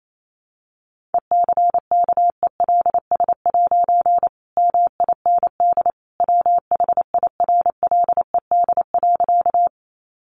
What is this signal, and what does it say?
Morse code, heard as a keyed tone, about 28 wpm, 700 Hz. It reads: ECKELS' MIND WHIRLED.
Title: A Sound Of Thunder 28wpm